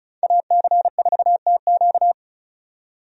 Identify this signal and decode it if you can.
Morse code, heard as a keyed tone, about 36 wpm, 700 Hz. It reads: AC4TQ
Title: USA Call Signs - 5 Characters - 35wpm